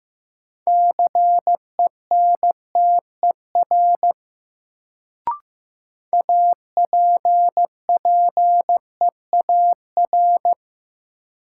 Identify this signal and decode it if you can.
Morse code, heard as a keyed tone, about 15 wpm, 700 Hz. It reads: CENTER APPEAR
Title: Sentences from Top 1000 Words - Review Part 2 - New Word Drill - 15wpm